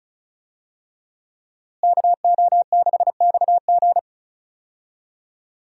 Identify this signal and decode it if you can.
Morse code, heard as a keyed tone, about 34 wpm, 700 Hz. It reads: KO6XG
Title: USA Call Signs - 5 Characters - Repeated 3x - 35wpm